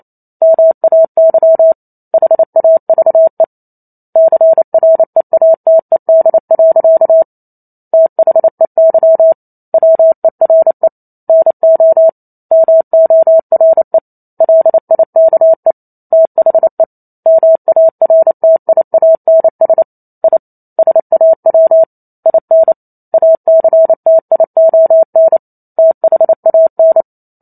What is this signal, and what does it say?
Morse code, heard as a keyed tone, about 28 wpm, 650 Hz. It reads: MAY HAVE CREATED. THEY WERE NO MORE LIKE THE MARTIANS I SAW IN ACTION THAN